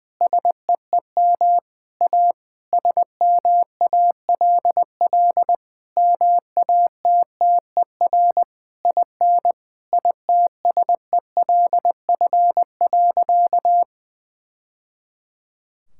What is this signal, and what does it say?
Morse code, heard as a keyed tone, about 20 wpm, 700 Hz. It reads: SEEM A SMALL MATTER IN ITSELF.
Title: Sentences from Top 1000 Words - Review Part 5 - 20wpm